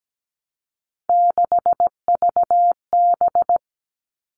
Morse code, heard as a keyed tone, about 17 wpm, 700 Hz. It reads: 6VB
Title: All Cut Numbers - 3-Digit 17wpm